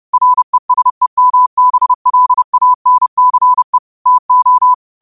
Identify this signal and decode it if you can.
Morse code, heard as a keyed tone, about 30 wpm, 1000 Hz. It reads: RESEMBLANCE TO